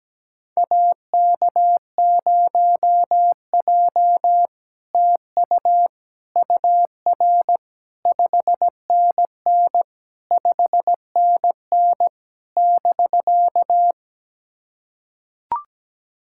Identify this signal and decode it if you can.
Morse code, heard as a keyed tone, about 17 wpm, 700 Hz. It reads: AK0J TU UR 5NN 5NN <BK>
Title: SOTA QSO Practice 17wpm